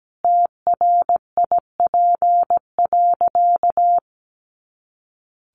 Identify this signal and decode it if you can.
Morse code, heard as a keyed tone, about 17 wpm, 700 Hz. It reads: TRIP.
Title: Additional Easy Phrases 17wpm